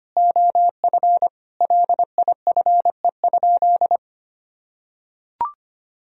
Morse code, heard as a keyed tone, about 25 wpm, 700 Hz. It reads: OF LIFE?
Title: Sentences from Top 1000 Words - Review Part 3 - 25wpm